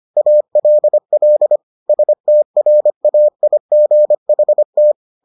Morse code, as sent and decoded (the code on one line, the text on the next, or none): .- .-.. .-.. / ... - .-. .- .. --. .... -
ALL STRAIGHT